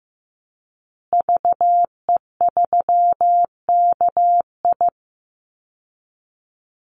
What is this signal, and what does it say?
Morse code, heard as a keyed tone, about 15 wpm, 700 Hz. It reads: VE3KI
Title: Call Signs Repeated 2x - 15wpm